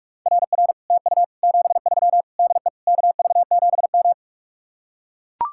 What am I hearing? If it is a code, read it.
Morse code, heard as a keyed tone, about 45 wpm, 700 Hz. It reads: RR TU 73 DE KV7M